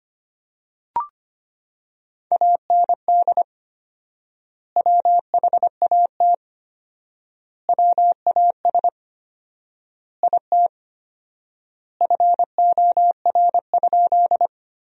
Morse code, heard as a keyed tone, about 25 wpm, 700 Hz. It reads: AND WHAT WAS IT FOR?
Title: 4x Word-Spacing - Sentences from Top 100 Words - 25wpm